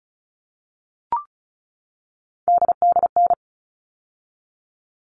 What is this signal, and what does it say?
Morse code, heard as a keyed tone, about 34 wpm, 700 Hz. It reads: DDN
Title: All Cut Numbers - 3-Digit 35wpm